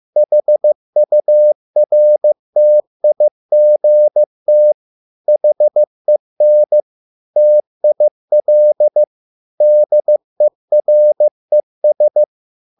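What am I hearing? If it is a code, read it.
Morse code, heard as a keyed tone, about 15 wpm, 600 Hz. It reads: HURTIGT HEN TIL DERES
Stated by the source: Morse code at a slow 15 wpm from CW med Gnister.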